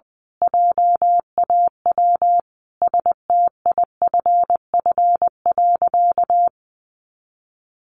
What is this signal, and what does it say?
Morse code, heard as a keyed tone, about 20 wpm, 700 Hz. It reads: JAW STIFF.
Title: A Sound Of Thunder 20wpm